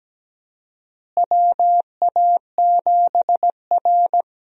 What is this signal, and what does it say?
Morse code, heard as a keyed tone, about 17 wpm, 700 Hz. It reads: WA7R
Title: USA Call Signs - 4 Characters - Repeated 2x - 17wpm